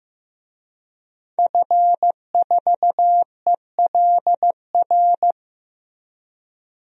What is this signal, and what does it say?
Morse code, heard as a keyed tone, about 15 wpm, 700 Hz. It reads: F4ELR
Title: DXCC-100 Call Signs - 5 Characters - Repeated 3x - 15wpm